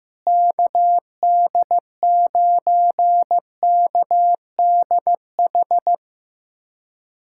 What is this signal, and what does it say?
Morse code, heard as a keyed tone, about 15 wpm, 700 Hz. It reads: KD9KDH
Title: USA Callsign DE Callsign - 6 Characters - 15wpm